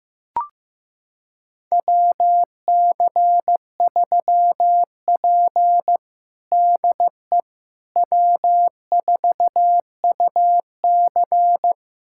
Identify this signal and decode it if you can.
Morse code, heard as a keyed tone, about 16 wpm, 700 Hz. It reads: WC3P DE W4UC
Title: USA Callsign DE Callsign - 4 Characters - 15wpm